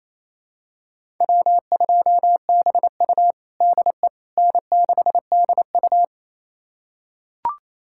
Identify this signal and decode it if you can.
Morse code, heard as a keyed tone, about 28 wpm, 700 Hz. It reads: W2BU DE N6DU
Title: USA Callsign DE Callsign - 4 Characters - 28wpm